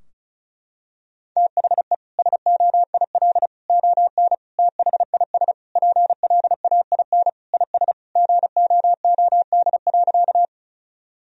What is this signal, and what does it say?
Morse code, heard as a keyed tone, about 36 wpm, 700 Hz. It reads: THE SOIL ON THIS PLAIN IS GOOD.
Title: Sentences from Top 1000 Words - Review Part 4 - 35wpm